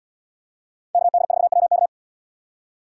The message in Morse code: KD4WK

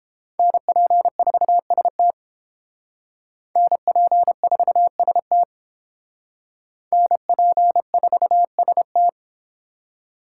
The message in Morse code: NP4ST NP4ST NP4ST

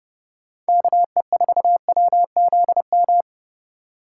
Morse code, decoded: KE4WZM

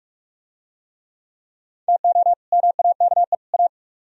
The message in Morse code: TO MAKE A

